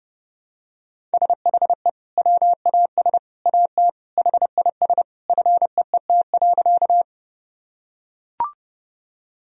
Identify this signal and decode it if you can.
Morse code, heard as a keyed tone, about 30 wpm, 700 Hz. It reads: SHE WAS AT HIS FEET.
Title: Sentences from Top 1000 Words - Review Part 1 - 30wpm